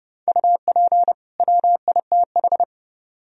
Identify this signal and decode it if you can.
Morse code, heard as a keyed tone, about 39 wpm, 700 Hz. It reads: UP WITH